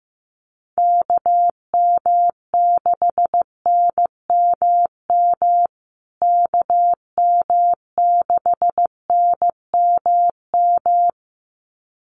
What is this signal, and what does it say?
Morse code, heard as a keyed tone, about 15 wpm, 700 Hz. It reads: KM6NMM KM6NMM
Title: USA Call Signs Repeated 2x 15wpm